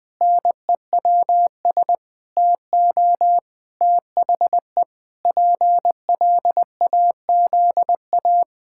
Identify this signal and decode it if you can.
Morse code, heard as a keyed tone, about 21 wpm, 700 Hz. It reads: NEWS TO THE PLAZA
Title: Pangram Challenge v2 - 20wpm